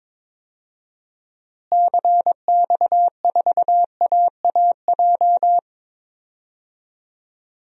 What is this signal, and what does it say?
Morse code, heard as a keyed tone, about 22 wpm, 700 Hz. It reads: CX4AAJ
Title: DXCC-100 Call Signs - 6 Characters - Repeated 3x - 22wpm